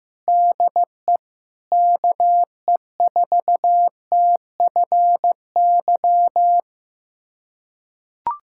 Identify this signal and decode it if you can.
Morse code, heard as a keyed tone, about 16 wpm, 700 Hz. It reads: DE KE4TFY
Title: USA Callsign DE Callsign 15wpm